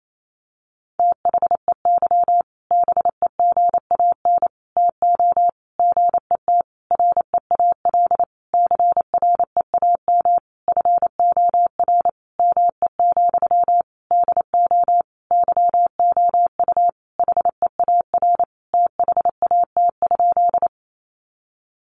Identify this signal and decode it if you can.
Morse code, heard as a keyed tone, about 28 wpm, 700 Hz. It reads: THEY BEGAN TO GET REAL CREAM FOR ME, DO YOU HEAR THAT?
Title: Sentences from Top 1500 Words 28wpm